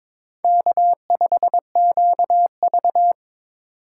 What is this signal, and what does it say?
Morse code, heard as a keyed tone, about 22 wpm, 700 Hz. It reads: K5QV
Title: USA Call Signs - 4 Characters - 6x Character Spacing - 22wpm